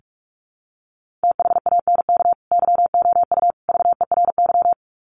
Morse code, heard as a keyed tone, about 44 wpm, 700 Hz. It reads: THANK YOU VERY
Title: Sets of 3 Words 45wpm